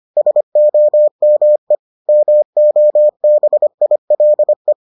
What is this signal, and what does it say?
Morse code, heard as a keyed tone, about 25 wpm, 600 Hz. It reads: SOME MOBILE